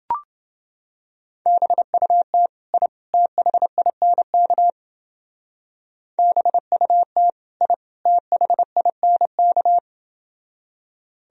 Morse code, heard as a keyed tone, about 29 wpm, 700 Hz. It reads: BUT I THINK BUT I THINK